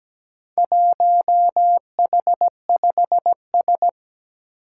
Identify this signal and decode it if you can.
Morse code, heard as a keyed tone, about 17 wpm, 700 Hz. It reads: 1H5S
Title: Four Challenging Letter-Number Combinations 17wpm